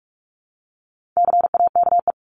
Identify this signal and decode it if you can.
Morse code, heard as a keyed tone, about 44 wpm, 700 Hz. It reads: CAKE
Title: Top 1501-1750 Words 45wpm